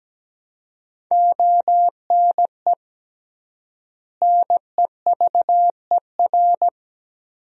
Morse code, heard as a keyed tone, about 17 wpm, 700 Hz. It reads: ONE NEVER
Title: 3x Word-Spacing - Sentences from Top 300 Words - 17wpm